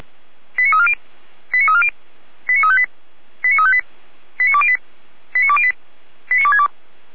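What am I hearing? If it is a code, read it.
Five-tone selective call (ZVEI), heard as quick runs of five tones, tones of about 80 ms. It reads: 89340 89340 89358 89358 89208 89208 89262
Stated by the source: Five-tone ZVEI selective call received by SDR, from a German fire brigade.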